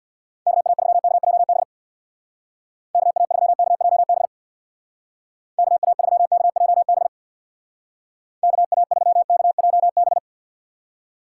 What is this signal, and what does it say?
Morse code, heard as a keyed tone, about 64 wpm, 700 Hz. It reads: KA3KJB KA3KJB KA3KJB KA3KJB